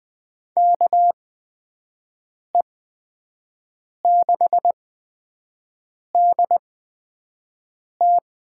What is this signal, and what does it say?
Morse code, heard as a keyed tone, about 20 wpm, spaced as for 5 wpm, 700 Hz. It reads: KE6DT
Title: USA Call Signs - 6 Characters - 8x Character Spacing - 20wpm